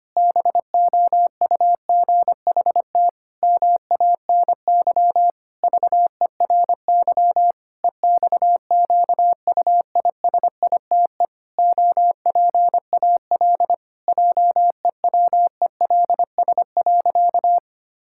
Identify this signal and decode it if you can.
Morse code, heard as a keyed tone, about 25 wpm, 700 Hz. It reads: BOUGHT MANY VERY EXQUISITE OPAL JEWELS.
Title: Pangram Challenge v2 - 25wpm